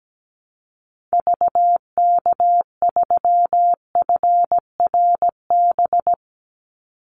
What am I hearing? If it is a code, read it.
Morse code, heard as a keyed tone, about 17 wpm, 700 Hz. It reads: VK3FRB
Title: USA Call Signs 17wpm